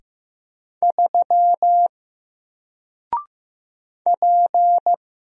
Morse code, heard as a keyed tone, about 15 wpm, 700 Hz. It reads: 3 P
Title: Single Letter-Number 15wpm